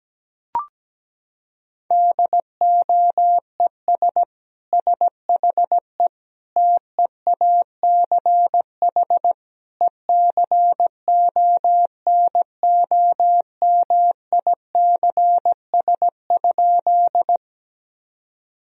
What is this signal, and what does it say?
Morse code, heard as a keyed tone, about 17 wpm, 700 Hz. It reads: DOES SHE TEACH ECONOMICS?
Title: Sentences from Top 1000 Words - Review Part 3 - 17wpm